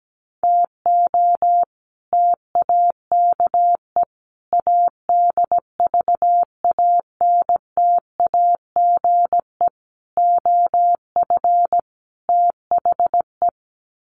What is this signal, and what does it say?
Morse code, heard as a keyed tone, about 17 wpm, 700 Hz. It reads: TO TAKE ADVANTAGE OF THE
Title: Sets of 5 Words 17wpm